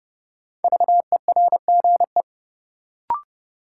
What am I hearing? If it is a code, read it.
Morse code, heard as a keyed tone, about 30 wpm, 700 Hz. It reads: VERGE
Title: Sets of 4 Words - New Word Drill - 30wpm